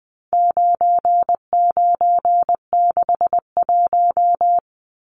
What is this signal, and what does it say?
Morse code, heard as a keyed tone, about 20 wpm, 700 Hz. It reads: 9961
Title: Numbers - 4-Digit 20wpm